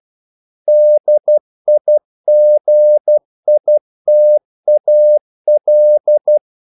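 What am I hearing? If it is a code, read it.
Morse code, heard as a keyed tone, about 12 wpm, 600 Hz. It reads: DIGITAL
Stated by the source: Morse practice sending English quotations.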